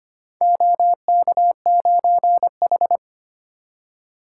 Morse code, heard as a keyed tone, about 25 wpm, 700 Hz. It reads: OK9H
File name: practice / 25wpm / Four Letter-Number Combinations - 25wpm